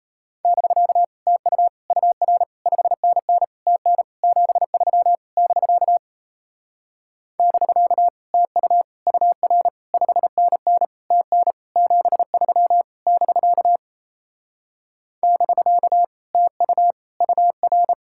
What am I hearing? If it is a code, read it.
Morse code, heard as a keyed tone, about 38 wpm, 700 Hz. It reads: <BK> TU UR 5NN TN 73 <BK> <BK> TU UR 5NN TN 73 <BK> <BK> TU UR